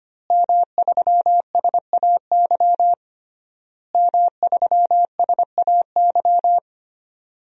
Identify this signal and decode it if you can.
Morse code, heard as a keyed tone, about 25 wpm, 700 Hz. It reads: M3SAY M3SAY
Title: DXCC-100 Call Signs - 5 Characters - Repeated 3x - 25wpm